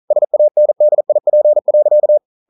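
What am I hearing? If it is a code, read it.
Morse code, heard as a keyed tone, about 42 wpm, 600 Hz. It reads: SANDIP.
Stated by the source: Morse code practice at 41 wpm, from CW med Gnister.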